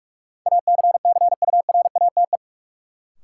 Morse code, heard as a keyed tone, about 46 wpm, 700 Hz. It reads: ACCURATE
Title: Sentences from Top 1200 Words - New Word Drill Alt - 45wpm